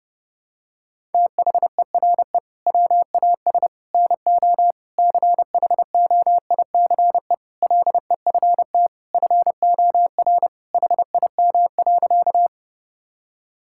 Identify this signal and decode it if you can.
Morse code, heard as a keyed tone, about 28 wpm, 700 Hz. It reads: THERE WAS NO CHOICE LEFT FOR HIM.
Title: Sentences from Top 1000 Words - Review Part 6 - 30wpm